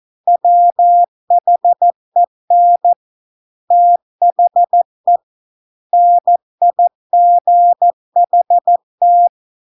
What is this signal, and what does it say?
Morse code, heard as a keyed tone, about 14 wpm, 700 Hz. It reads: WHEN THE NIGHT